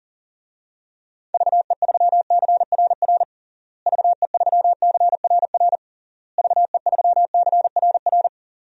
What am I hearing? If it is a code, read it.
Morse code, heard as a keyed tone, about 37 wpm, 700 Hz. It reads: VE3CRR VE3CRR VE3CRR
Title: DXCC-100 Call Signs - 6 Characters - Repeated 3x - 40wpm